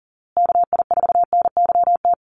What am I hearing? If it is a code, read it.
Morse code, heard as a keyed tone, about 40 wpm, 700 Hz. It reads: KI4NYT
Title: USA Call Signs Repeated 2x 40wpm